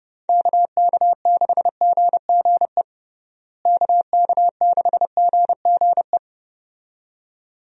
Morse code, heard as a keyed tone, about 30 wpm, 700 Hz. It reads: KK6GGE KK6GGE
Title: USA Call Signs - 6 Characters - Repeated 2x - 30wpm